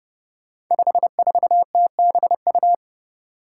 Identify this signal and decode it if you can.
Morse code, heard as a keyed tone, about 28 wpm, 700 Hz. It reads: 54TBU
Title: Five Challenging Letter-Number Combinations 30wpm